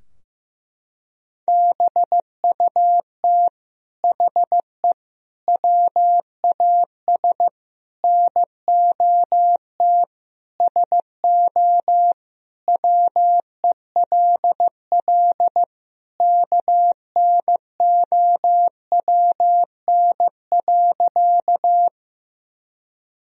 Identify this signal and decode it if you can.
Morse code, heard as a keyed tone, about 16 wpm, 700 Hz. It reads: BUT HE WAS NOT SO WELL KNOWN.
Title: Sentences from Top 1000 Words - Review Part 3 - 15wpm